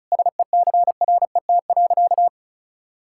Morse code, heard as a keyed tone, about 34 wpm, 700 Hz. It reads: SECRET.
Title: Sentences from Top 1600 Words 35wpm